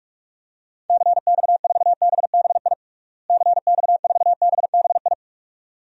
Morse code, heard as a keyed tone, about 46 wpm, 700 Hz. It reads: CX4BBI CX4BBI
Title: DXCC-100 Call Signs - 6 Characters - Repeated 2x - 45wpm